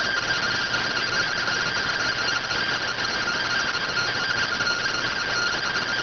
Two-tone teleprinter signal, about 170 Hz shift.